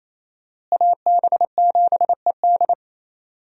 Morse code, heard as a keyed tone, about 28 wpm, 700 Hz. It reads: AB7ED
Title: USA Call Signs - 5 Characters - Repeated 3x - 28wpm